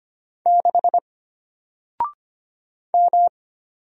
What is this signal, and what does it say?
Morse code, heard as a keyed tone, about 25 wpm, 700 Hz. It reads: 6 M
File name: Single-Letter-Number-25wpm